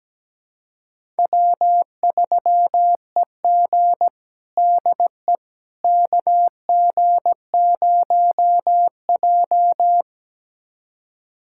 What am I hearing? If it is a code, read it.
Morse code, heard as a keyed tone, about 17 wpm, 700 Hz. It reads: W3EG DE KG0J
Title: USA Callsign DE Callsign - 4 Characters - 17wpm